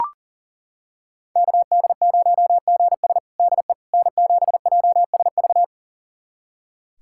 Morse code, heard as a keyed tone, about 40 wpm, 700 Hz. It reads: KD0GS DE N7JSV
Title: USA Callsign DE Callsign - 5 Characters - 40wpm